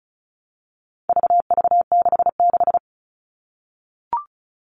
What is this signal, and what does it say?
Morse code, heard as a keyed tone, about 34 wpm, 700 Hz. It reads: VV66
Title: All Cut Numbers - 4-Digit 35wpm